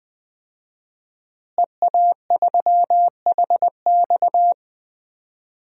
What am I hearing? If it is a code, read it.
Morse code, heard as a keyed tone, about 20 wpm, 700 Hz. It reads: EA3HX
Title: DXCC-100 Call Signs - 5 Characters - Repeated 3x - 20wpm